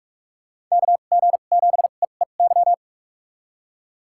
Morse code, heard as a keyed tone, about 46 wpm, spaced as for 32 wpm, 700 Hz. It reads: KG7EEY
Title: USA Call Signs - 6 Characters - 2x Character Spacing - 45wpm